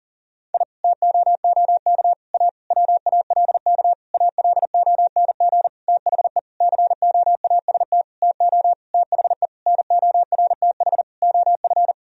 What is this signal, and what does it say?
Morse code, heard as a keyed tone, about 40 wpm, 690 Hz. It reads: I TOOK A WALK ALONG THE COAST TO THE NORTH OF
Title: Sentences from Top 1000 Words - Review Part 4 - 40wpm